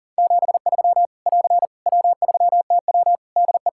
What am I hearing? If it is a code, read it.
Morse code, heard as a keyed tone, about 38 wpm, 700 Hz. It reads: 73 <AR> W3TW DE
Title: Practice QSO - Exchange 4 - Ending - 40wpm